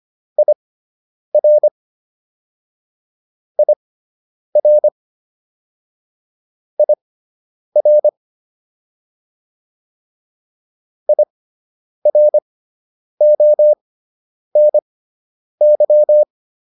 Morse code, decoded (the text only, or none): IR IR IR IRONY